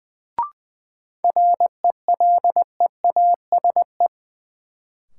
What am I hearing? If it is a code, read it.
Morse code, heard as a keyed tone, about 20 wpm, 700 Hz. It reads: RELEASE
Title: Sentences from Top 1100 Words - New Word Drill - 20wpm